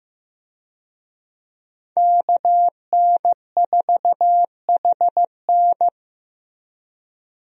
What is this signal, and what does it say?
Morse code, heard as a keyed tone, about 15 wpm, 700 Hz. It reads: KN4HN
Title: USA Call Signs - 5 Characters - Repeated 3x - 15wpm